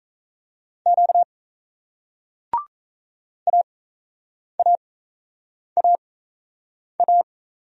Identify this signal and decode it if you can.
Morse code, heard as a keyed tone, about 38 wpm, spaced as for 8 wpm, 700 Hz. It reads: Q AAAA